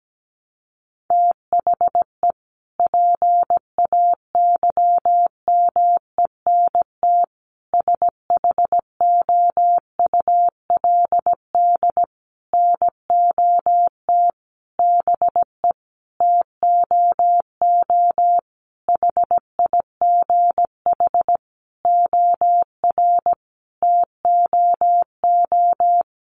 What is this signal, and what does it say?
Morse code, heard as a keyed tone, about 17 wpm, 700 Hz. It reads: THE PAYMENT SHOULD NOT BE TOO HIGH OR TOO
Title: Sentences from Top 1200 Words 17wpm